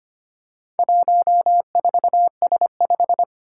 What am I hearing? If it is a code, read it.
Morse code, heard as a keyed tone, about 25 wpm, 700 Hz. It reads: 14S5
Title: Four Challenging Letter-Number Combinations 25wpm